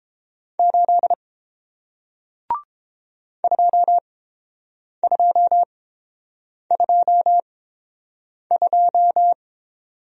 Morse code, heard as a keyed tone, about 30 wpm, spaced as for 7 wpm, 700 Hz. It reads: 8 2222